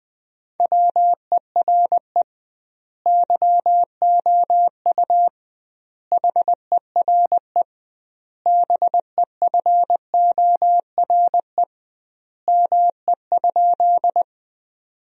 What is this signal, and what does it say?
Morse code, heard as a keyed tone, about 20 wpm, 700 Hz. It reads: WERE YOU HERE BEFORE ME?
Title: 2x Word-Spacing - Sentences from Top 200 Words - 20wpm